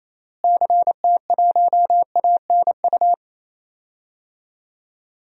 Morse code, heard as a keyed tone, about 28 wpm, 700 Hz. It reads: CT1ANU
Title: DXCC-100 Call Signs - 6 Characters - Repeated 3x - 28wpm